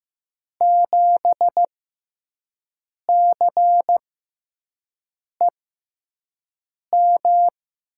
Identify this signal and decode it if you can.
Morse code, heard as a keyed tone, about 15 wpm, spaced as for 5 wpm, 700 Hz. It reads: 7CEM
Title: USA Call Signs - 5 Characters - 6x Character Spacing - 15wpm